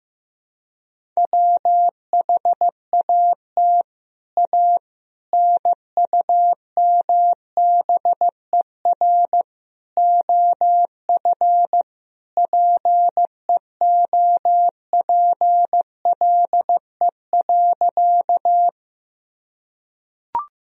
Morse code, heard as a keyed tone, about 15 wpm, 700 Hz. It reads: WHAT A NUMBER OF PEOPLE.
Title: Sentences from Top 1000 Words - Review Part 1 - 15wpm